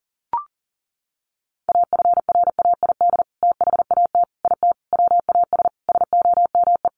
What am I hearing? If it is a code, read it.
Morse code, heard as a keyed tone, about 40 wpm, 700 Hz. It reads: AFRAID THAT IT WAS SOME
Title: Sentences from Top 1500 Words 40wpm